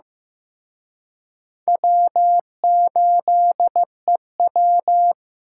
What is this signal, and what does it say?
Morse code, heard as a keyed tone, about 18 wpm, 700 Hz. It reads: W8EW
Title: Speed Racing - USA 1x2 Call Signs 15wpm